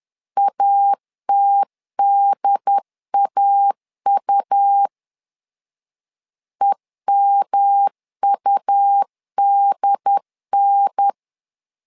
Morse code, heard as a keyed tone, about 10 wpm, 800 Hz. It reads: ATDAU EMUDN
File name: LEZIONI / 3 MWRO / 3 M 10wpm 1828